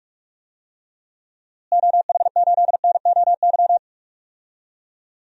Morse code, heard as a keyed tone, about 46 wpm, 700 Hz. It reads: OH8NOY
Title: DXCC-100 Call Signs - 6 Characters - Repeated 3x - 45wpm